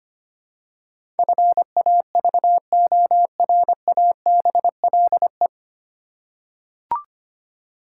Morse code, heard as a keyed tone, about 25 wpm, 700 Hz. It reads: FAVORABLE
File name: practice / 25wpm / 9 Letter Words - First 3 Letters Given - 25wpm